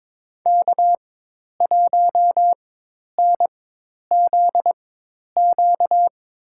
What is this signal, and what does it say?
Morse code, heard as a keyed tone, about 21 wpm, spaced as for 10 wpm, 700 Hz. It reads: K1NZQ
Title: USA Call Signs - 5 Characters - 4x Character Spacing - 22wpm